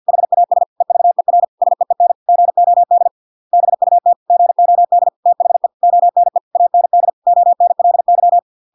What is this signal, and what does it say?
Morse code, heard as a keyed tone, about 50 wpm, 700 Hz. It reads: HAS EVER SEEN GOD BUT GOD THE ONE AND ONLY